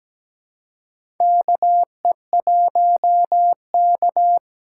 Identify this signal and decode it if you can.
Morse code, heard as a keyed tone, about 17 wpm, 700 Hz. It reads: KE1K
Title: USA Call Signs - 4 Characters - 3x Character Spacing - 17wpm